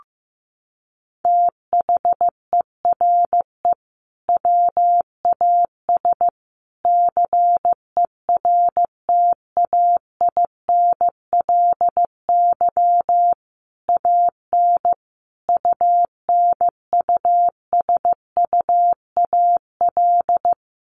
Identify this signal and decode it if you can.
Morse code, heard as a keyed tone, about 15 wpm, 700 Hz. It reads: THERE WAS CERTAINLY AN UNUSUAL
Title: Sentences from Top 1400 Words 15wpm